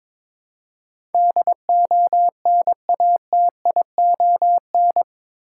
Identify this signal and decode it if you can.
Morse code, heard as a keyed tone, about 22 wpm, 700 Hz. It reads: DONATION
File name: practice / 22wpm / Eight Letter Words - 22wpm